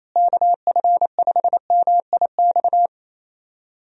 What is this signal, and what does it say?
Morse code, heard as a keyed tone, about 28 wpm, 700 Hz. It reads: KF5MIX
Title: USA Call Signs - 6 Characters - Repeated 3x - 28wpm